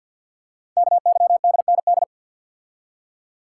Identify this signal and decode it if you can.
Morse code, heard as a keyed tone, about 50 wpm, 700 Hz. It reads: KYDND
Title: Code Groups - 5 Letters - 50wpm